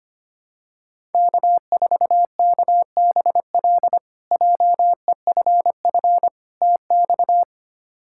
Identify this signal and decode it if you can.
Morse code, heard as a keyed tone, about 25 wpm, 700 Hz. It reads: K4KBL JEFF TX
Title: K1USN SST QSO Practice - 25wpm